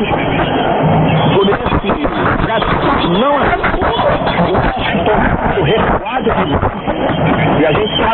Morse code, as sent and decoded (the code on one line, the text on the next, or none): none